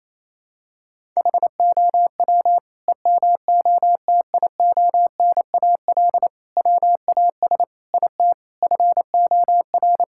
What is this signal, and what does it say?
Morse code, heard as a keyed tone, about 27 wpm, 700 Hz. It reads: HOW EMOTIONAL WAS IT FOR
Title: Sentences from Top 1000 Words - Review Part 5 - 28wpm